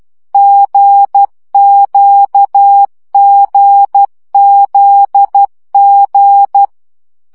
Morse code, heard as a keyed tone, about 12 wpm, 800 Hz. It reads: GQGZG
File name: CW gyakorlas g z q 10 baud